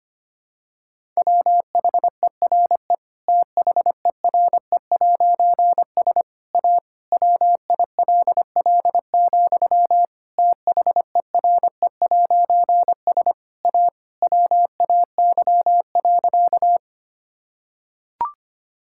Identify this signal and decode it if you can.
Morse code, heard as a keyed tone, about 25 wpm, 700 Hz. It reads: WHERE THERE'S A WILL, THERE'S A WAY.
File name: practice / 25wpm / Difficult Sentences with Hints - 25wpm